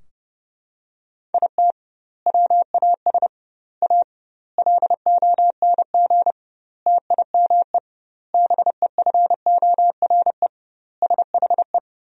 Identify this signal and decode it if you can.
Morse code, heard as a keyed tone, about 30 wpm, 700 Hz. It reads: IT WAS A LONG TIME BEFORE SHE
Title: 2x Word-Spacing - Sentences from Top 500 Words - 30wpm